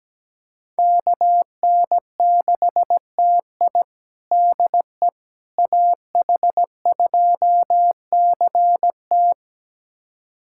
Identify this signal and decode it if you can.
Morse code, heard as a keyed tone, about 17 wpm, 700 Hz. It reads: KN6TI DE AH2CT
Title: USA Callsign DE Callsign - 5 Characters - 17wpm